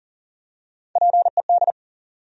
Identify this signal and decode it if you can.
Morse code, heard as a keyed tone, about 40 wpm, 700 Hz. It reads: PED
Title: USA Call Sign Postfixes - 3 Letters 40wpm